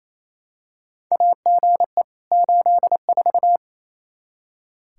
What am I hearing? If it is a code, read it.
Morse code, heard as a keyed tone, about 28 wpm, 700 Hz. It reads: AGE 84
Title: Practice QSO - Exchange 3 - More Info - 28wpm